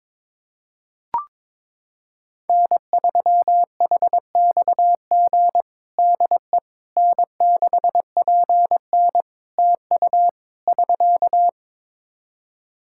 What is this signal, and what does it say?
Morse code, heard as a keyed tone, about 22 wpm, 700 Hz. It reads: N3HXG DE N6PN TU <SK>
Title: K1USN SST QSO Practice 22wpm